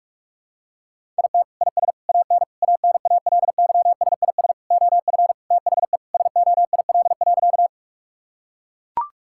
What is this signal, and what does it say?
Morse code, heard as a keyed tone, about 41 wpm, 700 Hz. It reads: IT IS AN ANALYSIS OF THE SOIL.
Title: Sentences from Top 1000 Words - Review Part 5 - 45wpm